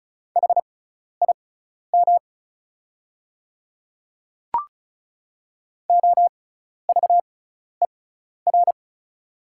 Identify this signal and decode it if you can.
Morse code, heard as a keyed tone, about 35 wpm, spaced as for 12 wpm, 690 Hz. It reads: HIM OVER
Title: 6x Character Spacing - Top 100 Words - 35wpm